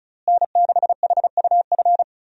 Morse code, heard as a keyed tone, about 34 wpm, 700 Hz. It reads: N6HUF
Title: USA Call Signs - 5 Characters - Repeated 2x - 35wpm